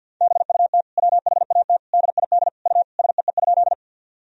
Morse code, heard as a keyed tone, about 50 wpm, 700 Hz. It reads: BUT WHAT DID U SEE?
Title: Sentences from Top 100 Words and Abbreviations 50wpm